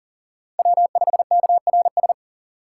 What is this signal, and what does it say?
Morse code, heard as a keyed tone, about 40 wpm, 700 Hz. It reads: W5KRS
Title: USA Call Signs - 6x Character Spacing - 40wpm